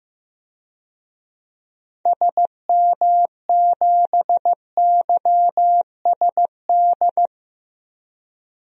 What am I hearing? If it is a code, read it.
Morse code, heard as a keyed tone, about 15 wpm, 690 Hz. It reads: SM7YSD
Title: DXCC-100 Call Signs - 6 Characters - 15wpm